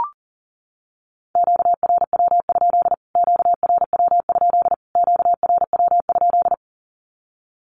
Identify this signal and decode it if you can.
Morse code, heard as a keyed tone, about 40 wpm, 700 Hz. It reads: QRW? QRW? QRW?
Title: Q-Codes Repeated 3x 40wpm